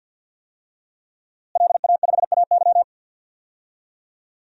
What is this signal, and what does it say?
Morse code, heard as a keyed tone, about 46 wpm, 700 Hz. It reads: LA5AY
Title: DXCC-100 Call Signs - 5 Characters - Repeated 2x - 50wpm